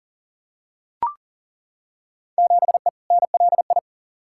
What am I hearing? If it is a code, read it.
Morse code, heard as a keyed tone, about 40 wpm, 700 Hz. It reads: 7E NLI
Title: Field Day Exchanges Encore - 40wpm